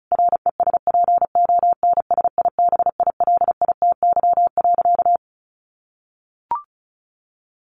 Morse code, heard as a keyed tone, about 34 wpm, 700 Hz. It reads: RESPONSIBILITY.
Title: Sentences from Top 1200 Words 35wpm